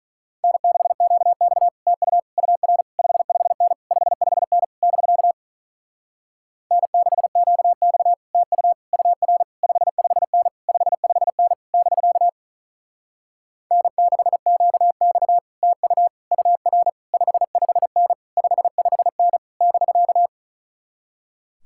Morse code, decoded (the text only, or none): N6QX TU UR 55N 55N <BK> N6QX TU UR 55N 55N <BK> N6QX TU UR 55N 55N <BK>